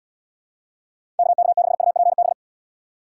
Morse code, heard as a keyed tone, about 70 wpm, 700 Hz. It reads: KK6UPB